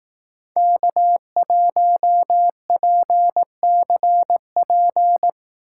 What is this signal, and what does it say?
Morse code, heard as a keyed tone, about 18 wpm, 700 Hz. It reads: K1PCP
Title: Speed Racing - USA Call Signs 15wpm